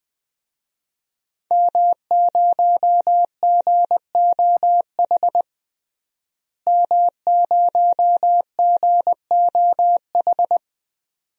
Morse code, heard as a keyed tone, about 20 wpm, 700 Hz. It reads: M0GOH M0GOH
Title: DXCC-100 Call Signs - 5 Characters - Repeated 2x - 20wpm